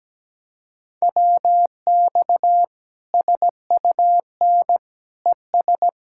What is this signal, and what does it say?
Morse code, heard as a keyed tone, about 17 wpm, 700 Hz. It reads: WX SUN ES
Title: K1USN SST QSO Practice 17wpm